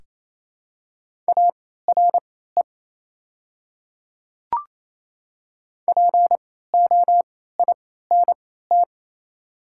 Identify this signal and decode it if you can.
Morse code, heard as a keyed tone, about 28 wpm, spaced as for 16 wpm, 700 Hz. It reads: ARE POINT
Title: 3x Character Spacing - Top 200 Words - 28wpm